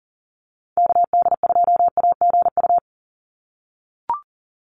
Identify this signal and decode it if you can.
Morse code, heard as a keyed tone, about 40 wpm, 700 Hz. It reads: KD2AGU
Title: USA Call Signs 40wpm